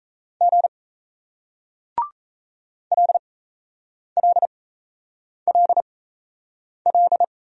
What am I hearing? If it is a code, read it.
Morse code, heard as a keyed tone, about 38 wpm, spaced as for 8 wpm, 700 Hz. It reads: G LLLL